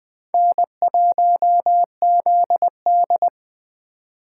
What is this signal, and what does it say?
Morse code, heard as a keyed tone, about 20 wpm, 700 Hz. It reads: N1ZD
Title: Speed Racing - USA 1x2 Call Signs 17wpm